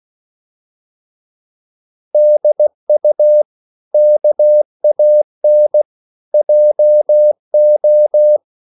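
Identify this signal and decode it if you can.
Morse code, heard as a keyed tone, about 16 wpm, 600 Hz. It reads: DU KAN JO
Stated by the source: Slow (QRS) Morse code at 16 wpm, from CW med Gnister.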